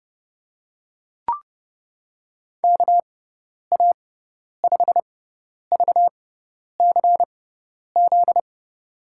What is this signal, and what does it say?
Morse code, heard as a keyed tone, about 31 wpm, spaced as for 10 wpm, 700 Hz. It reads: KA5VCZ
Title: USA Call Signs - 6x Character Spacing - 30wpm